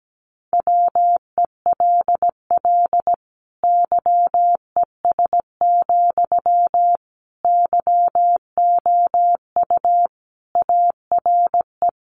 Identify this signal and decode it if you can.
Morse code, heard as a keyed tone, about 17 wpm, 700 Hz. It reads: WELL YES, YOU ARE
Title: Sentences from Top 500 Words 17wpm